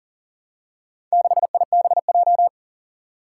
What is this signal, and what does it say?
Morse code, heard as a keyed tone, about 40 wpm, 700 Hz. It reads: 6IBJ
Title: Four Challenging Letter-Number Combinations 40wpm